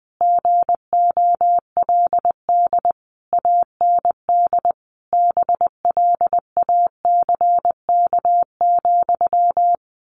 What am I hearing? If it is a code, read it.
Morse code, heard as a keyed tone, about 20 wpm, 700 Hz. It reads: GOLD AND BLACK,
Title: A Sound Of Thunder 20wpm